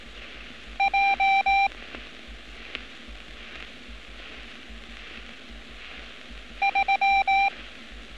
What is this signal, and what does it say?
Morse code, heard as a keyed tone, about 19 wpm, 790 Hz. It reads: J 3
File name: training / NAVPERS 11200-11211 / NAVPERS 11206 RA RB